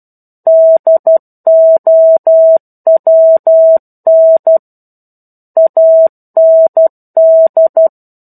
Morse code, heard as a keyed tone, about 12 wpm, 650 Hz. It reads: DOWN AND
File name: 01-Chapter-01 12 WPM